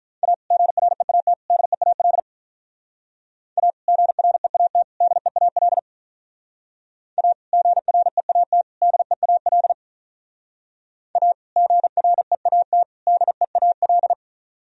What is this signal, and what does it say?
Morse code, heard as a keyed tone, about 44 wpm, 700 Hz. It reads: A GREAT DEAL A GREAT DEAL A GREAT DEAL A GREAT DEAL